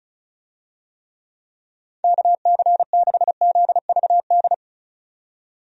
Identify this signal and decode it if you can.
Morse code, heard as a keyed tone, about 36 wpm, 700 Hz. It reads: KC6ZVD
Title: USA Call Signs - 6 Characters - 35wpm